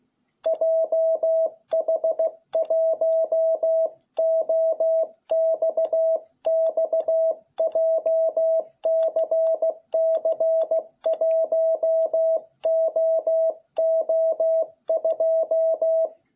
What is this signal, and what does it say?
Morse code, heard as a keyed tone, about 16 wpm, 660 Hz. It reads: JH1OXX JCC1OO2